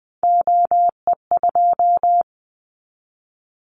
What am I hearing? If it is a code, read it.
Morse code, heard as a keyed tone, about 20 wpm, 700 Hz. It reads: OE2
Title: Callsign Prefixes for Top 100 DXCC - 20wpm